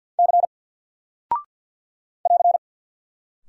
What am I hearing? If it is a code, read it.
Morse code, heard as a keyed tone, about 48 wpm, 700 Hz. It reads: C <AR>
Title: Single Character (Letter:Number:Punctuation:Prosign) Alt 50wpm